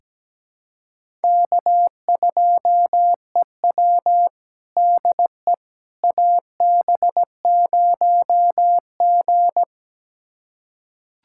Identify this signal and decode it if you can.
Morse code, heard as a keyed tone, about 17 wpm, 700 Hz. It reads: K2EW DE AB0G
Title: USA Callsign DE Callsign - 4 Characters - 17wpm